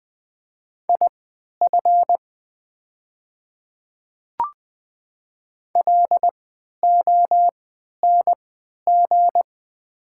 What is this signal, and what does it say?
Morse code, heard as a keyed tone, about 20 wpm, spaced as for 11 wpm, 700 Hz. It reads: IF LONG
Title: 3x Character Spacing - Top 100 Words - 20wpm